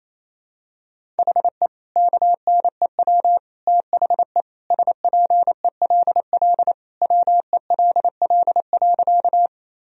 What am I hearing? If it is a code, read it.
Morse code, heard as a keyed tone, about 28 wpm, 700 Hz. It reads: HE KNEW THE SPELL WELL.
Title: Sentences from Top 1000 Words - Review Part 1 - 28wpm